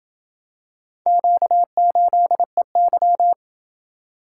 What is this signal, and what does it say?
Morse code, heard as a keyed tone, about 27 wpm, 700 Hz. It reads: Q8EY